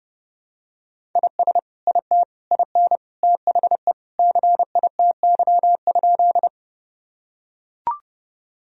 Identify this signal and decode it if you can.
Morse code, heard as a keyed tone, about 32 wpm, 700 Hz. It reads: IS IT IN THE CITY?
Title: Sentences from Top 1000 Words - Review Part 1 - 30wpm